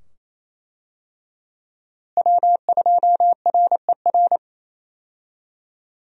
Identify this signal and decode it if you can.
Morse code, heard as a keyed tone, about 27 wpm, 700 Hz. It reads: W2RER
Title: USA Call Signs - 5 Characters - Repeated 2x - 28wpm